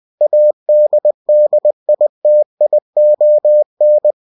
Morse code, heard as a keyed tone, about 20 wpm, 600 Hz. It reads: ADDITION
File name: cw-books / winnie-the-pooh / winnie-20wpm-page-0114